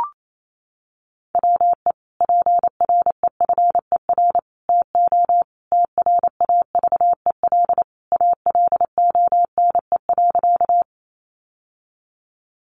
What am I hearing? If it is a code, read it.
Morse code, heard as a keyed tone, about 28 wpm, 700 Hz. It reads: WE PREFER TO TRAVEL ALONE.
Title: Sentences from Top 1100 Words 28wpm